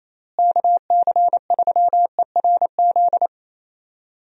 Morse code, heard as a keyed tone, about 28 wpm, 700 Hz. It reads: KC3ERZ
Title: USA Call Signs - 6 Characters - Repeated 3x - 28wpm